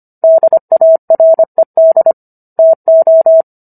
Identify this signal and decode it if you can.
Morse code, heard as a keyed tone, about 25 wpm, 650 Hz. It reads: DARED TO
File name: Book One / 16-Chapter 16 - 25 WPM